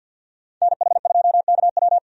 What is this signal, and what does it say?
Morse code, heard as a keyed tone, about 47 wpm, 700 Hz. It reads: NH2KW
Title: USA Call Signs - 5 Characters - 50wpm